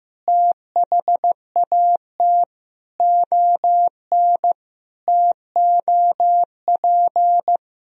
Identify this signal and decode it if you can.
Morse code, heard as a keyed tone, about 15 wpm, 700 Hz. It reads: THAT ON TOP
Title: Sentences from Top 1000 Words - Review Part 2 - 15wpm